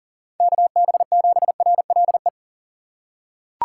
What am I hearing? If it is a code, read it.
Morse code, heard as a keyed tone, about 40 wpm, 700 Hz. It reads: KB7RLE
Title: USA Call Signs - 4x Character Spacing - 40wpm